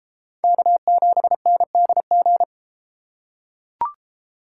Morse code, heard as a keyed tone, about 33 wpm, 700 Hz. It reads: K7NDG